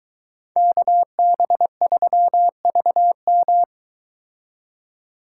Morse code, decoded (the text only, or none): KB3VM